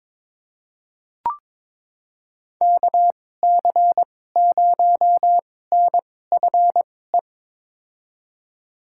Morse code, heard as a keyed tone, about 22 wpm, spaced as for 16 wpm, 700 Hz. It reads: KC0NFE
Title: USA Call Signs - 6 Characters - 2x Character Spacing - 22wpm